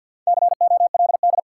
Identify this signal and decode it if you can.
Morse code, heard as a keyed tone, about 53 wpm, 700 Hz. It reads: COLD
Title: Sentences from Top 400 Words - New Word Drill - 50wpm